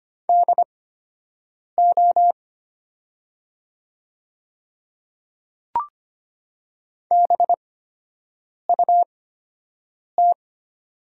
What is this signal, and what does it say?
Morse code, heard as a keyed tone, about 25 wpm, spaced as for 7 wpm, 700 Hz. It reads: DO BUT
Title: 8x Character Spacing - Top 100 Words - 25wpm